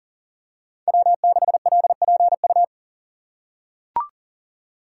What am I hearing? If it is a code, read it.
Morse code, heard as a keyed tone, about 40 wpm, 700 Hz. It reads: W6LPU
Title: USA Call Signs - 2x Character Spacing - 40wpm